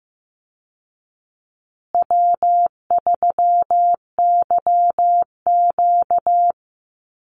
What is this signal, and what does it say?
Morse code, heard as a keyed tone, about 15 wpm, 700 Hz. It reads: W3YQ
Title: Call Signs Repeated 2x - 15wpm